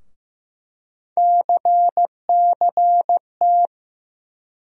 Morse code, heard as a keyed tone, about 15 wpm, 700 Hz. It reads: CCT
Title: USA Call Sign Postfixes - 3 Letters 15wpm